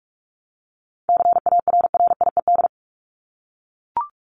Morse code, heard as a keyed tone, about 44 wpm, 700 Hz. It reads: CARRIED
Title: ED Suffix - Top 100 Words with Base Words 45wpm